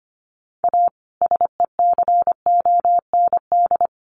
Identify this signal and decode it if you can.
Morse code, heard as a keyed tone, about 25 wpm, 700 Hz. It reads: A SECOND
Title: Sentences from Top 1100 Words 25wpm